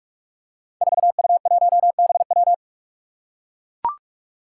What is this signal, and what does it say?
Morse code, heard as a keyed tone, about 44 wpm, 700 Hz. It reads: 4U1BW
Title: Five Challenging Letter-Number Combinations 45wpm